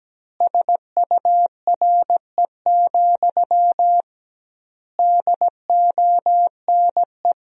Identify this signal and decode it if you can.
Morse code, heard as a keyed tone, about 17 wpm, 700 Hz. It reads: SURE, DONE
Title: 2x Word-Spacing - Sentences from Top 500 Words - 17wpm